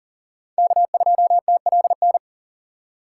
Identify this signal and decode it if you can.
Morse code, heard as a keyed tone, about 40 wpm, 700 Hz. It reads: K2TLN
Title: USA Call Signs - 4x Character Spacing - 40wpm